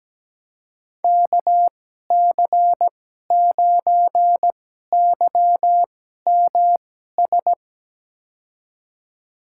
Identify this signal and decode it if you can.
Morse code, heard as a keyed tone, about 17 wpm, spaced as for 12 wpm, 700 Hz. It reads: KC9YMS
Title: USA Call Signs - 2x Character Spacing - 17wpm